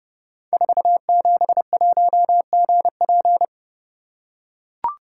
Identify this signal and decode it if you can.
Morse code, heard as a keyed tone, about 30 wpm, 700 Hz. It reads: 471GP
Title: Code Groups - 5 Letter:Numbers - 30wpm